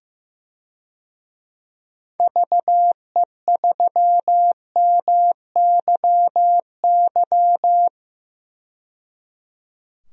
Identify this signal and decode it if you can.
Morse code, heard as a keyed tone, about 15 wpm, 700 Hz. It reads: VE3MYY
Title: DXCC-100 Call Signs - 6 Characters - 15wpm